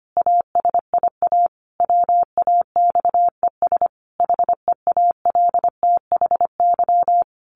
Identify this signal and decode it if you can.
Morse code, heard as a keyed tone, about 25 wpm, 700 Hz. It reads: ASIA WAXES HEALTHY
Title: A Sound Of Thunder 25wpm